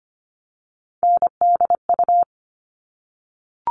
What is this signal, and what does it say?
Morse code, heard as a keyed tone, about 25 wpm, 700 Hz. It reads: NDU
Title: All Cut Numbers - 3-Digit 25wpm